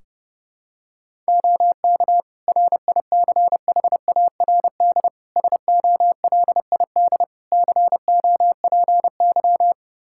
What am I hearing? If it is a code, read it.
Morse code, heard as a keyed tone, about 30 wpm, 700 Hz. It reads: OK RICHARD SOLID COPY
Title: Practice QSO - Exchange 3 - More Info - 30wpm